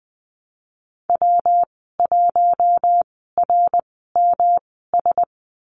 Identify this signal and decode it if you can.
Morse code, heard as a keyed tone, about 20 wpm, spaced as for 15 wpm, 700 Hz. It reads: W1RMS
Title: 128 Lesson 21g - Number 6 - Call Signs 20wpm